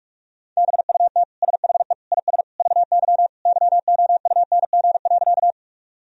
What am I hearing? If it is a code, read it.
Morse code, heard as a keyed tone, about 46 wpm, 700 Hz. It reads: BUT SHE IS VY YOUNG.
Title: Sentences from Top 400 Words and Abbreviations 45wpm